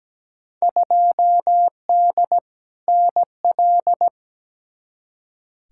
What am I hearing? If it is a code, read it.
Morse code, heard as a keyed tone, about 17 wpm, 700 Hz. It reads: 2D NL
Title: Field Day Exchanges Encore - 17wpm